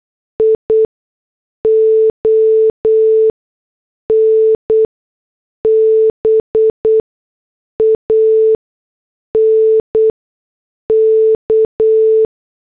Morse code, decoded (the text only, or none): ION BANK